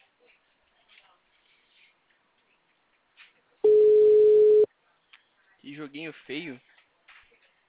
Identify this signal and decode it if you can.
Ringback tone (Europe, 425 Hz), with speech.
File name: out- -2001- - - .2129